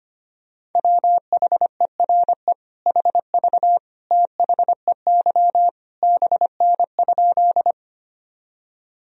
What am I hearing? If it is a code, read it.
Morse code, heard as a keyed tone, about 25 wpm, 700 Hz. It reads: WHERE HV THEY BN?
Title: Sentences from Top 200 Words and Abbreviations 25wpm